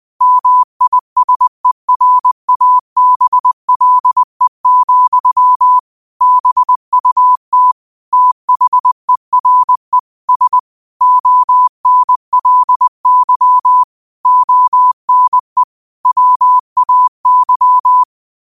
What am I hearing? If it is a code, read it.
Morse code, heard as a keyed tone, about 20 wpm, 1000 Hz. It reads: MISERABLE, BUT THERE S ONLY ONE WAY
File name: qotdfiles / 20wpm